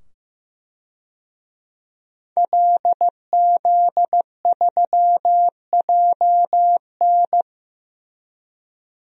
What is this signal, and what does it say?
Morse code, heard as a keyed tone, about 15 wpm, 700 Hz. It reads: LZ3JN
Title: DXCC-100 Call Signs - 5 Characters - 15wpm